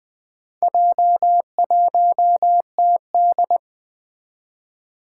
Morse code, heard as a keyed tone, about 20 wpm, 700 Hz. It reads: J1TD
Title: Four Challenging Letter-Number Combinations 20wpm